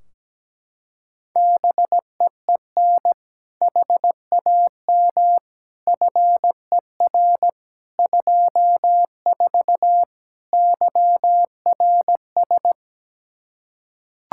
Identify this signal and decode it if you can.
Morse code, heard as a keyed tone, about 17 wpm, 700 Hz. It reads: BEEN HAM FER 24 YRS
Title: Practice QSO - Exchange 3 - More Info - 17wpm